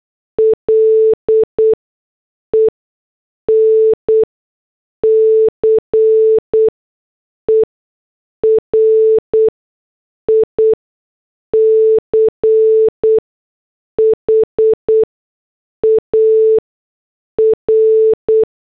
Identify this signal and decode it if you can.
Morse code, heard as a keyed tone, about 8 wpm, spaced as for 6 wpm, 440 Hz. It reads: LENCE RICHAR